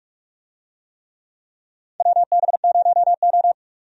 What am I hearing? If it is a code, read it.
Morse code, heard as a keyed tone, about 46 wpm, 700 Hz. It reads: WB0O
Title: USA Call Signs - 4 Characters - 45wpm